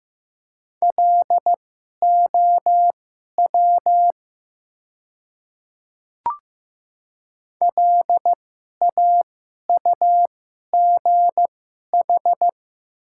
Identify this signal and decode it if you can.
Morse code, heard as a keyed tone, about 15 wpm, spaced as for 11 wpm, 700 Hz. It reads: LOW LAUGH
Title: 2x Character Spacing - Top 500 Words - 15wpm